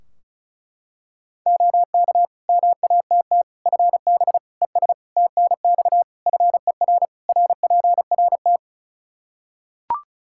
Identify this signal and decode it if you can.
Morse code, heard as a keyed tone, about 36 wpm, 700 Hz. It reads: OK MATT FB ES TNX FER RPRT
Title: Practice QSO - Exchange 2 - Info - 35wpm